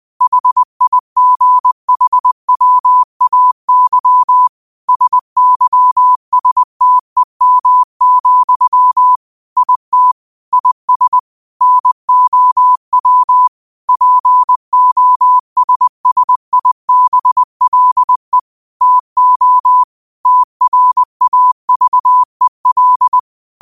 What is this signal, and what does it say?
Morse code, heard as a keyed tone, about 20 wpm, 1000 Hz. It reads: HIGHWAY SYSTEM, IT IS NOW POSSIBLE TO TRAVEL
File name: qotdfiles / 20wpm